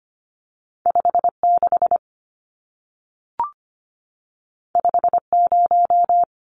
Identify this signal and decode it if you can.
Morse code, heard as a keyed tone, about 25 wpm, 700 Hz. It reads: 56 50
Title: Numbers - 2-Digit 25wpm